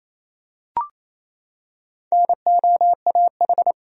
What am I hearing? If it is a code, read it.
Morse code, heard as a keyed tone, about 28 wpm, 700 Hz. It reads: NOAH
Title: USA Common Names - Max 4 Letters - 28wpm